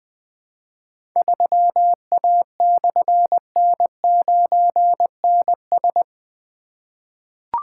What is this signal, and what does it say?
Morse code, heard as a keyed tone, about 20 wpm, 700 Hz. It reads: 3A/N9NS
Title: Difficult Call Signs 20wpm